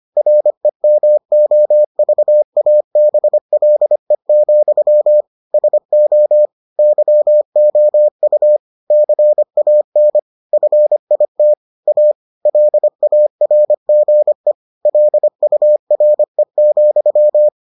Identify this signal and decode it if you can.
Morse code, heard as a keyed tone, about 25 wpm, 600 Hz. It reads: REMOVABLE, SO YOU CAN FIT A LARGE LURE,